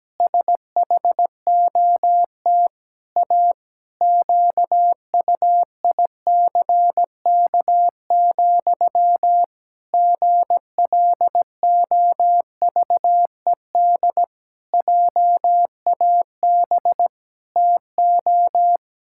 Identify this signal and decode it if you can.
Morse code, heard as a keyed tone, about 17 wpm, 700 Hz. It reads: SHOT A QUICK, GLOVED JAB TO
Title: Pangram Challenge v2 - 17wpm